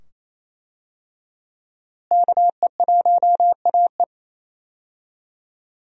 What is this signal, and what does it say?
Morse code, heard as a keyed tone, about 28 wpm, 700 Hz. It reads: KE1AE
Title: USA Call Signs - 5 Characters - Repeated 2x - 28wpm